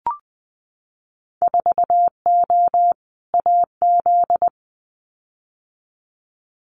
Morse code, heard as a keyed tone, about 20 wpm, 700 Hz. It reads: 4O AZ
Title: Winter Field Day Exchanges Encore 20wpm